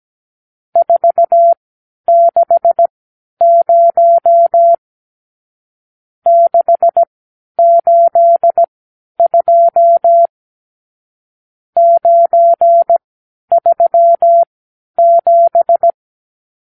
Morse code, heard as a keyed tone, about 17 wpm, spaced as for 10 wpm, 680 Hz. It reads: 460 682 937
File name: arrl / Review (1-0)